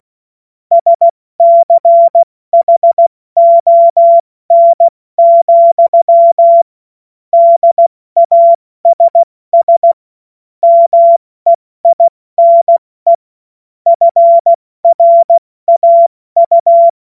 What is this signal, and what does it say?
Morse code, heard as a keyed tone, about 16 wpm, 680 Hz. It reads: SCHON, DASS MEINE FRAU
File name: Morsekurs / Witze / Witz 09. Dezember 2025 70 F80